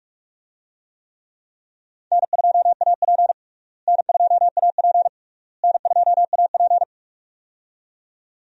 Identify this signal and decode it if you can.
Morse code, heard as a keyed tone, about 42 wpm, 700 Hz. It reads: N2AP N2AP N2AP
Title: USA Call Signs - 4 Characters - Repeated 3x - 45wpm